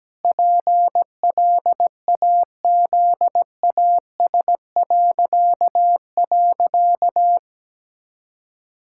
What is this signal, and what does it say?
Morse code, heard as a keyed tone, about 17 wpm, 700 Hz. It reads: PLAZAS..
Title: Pangram Challenge v2 - 17wpm